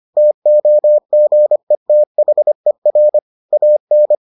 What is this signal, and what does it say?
Morse code, heard as a keyed tone, about 25 wpm, 600 Hz. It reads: TOGETHER AN